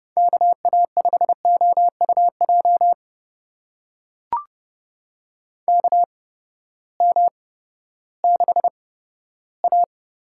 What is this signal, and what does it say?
Morse code, heard as a keyed tone, about 30 wpm, spaced as for 15 wpm, 700 Hz. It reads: KA5OUJ KM6A
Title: USA Call Signs - 6 Characters - 8x Character Spacing - 30wpm